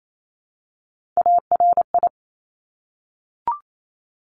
Morse code, heard as a keyed tone, about 28 wpm, 700 Hz. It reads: ARI
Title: Top Three Letter Combinations in Words 28wpm